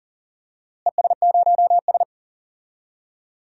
Morse code, heard as a keyed tone, about 40 wpm, 700 Hz. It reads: ES0S
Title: DXCC-100 Call Signs - 4 Characters - Repeated 3x - 40wpm